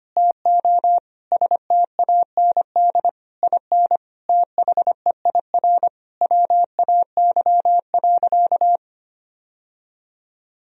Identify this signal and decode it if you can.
Morse code, heard as a keyed tone, about 25 wpm, 700 Hz. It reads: TO STAND IN THEIR WAY.
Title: Sentences from Top 1000 Words - Review Part 1 - 25wpm